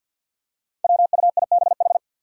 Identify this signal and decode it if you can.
Morse code, heard as a keyed tone, about 50 wpm, 700 Hz. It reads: WUIBH
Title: Five Challenging Letter-Number Combinations 50wpm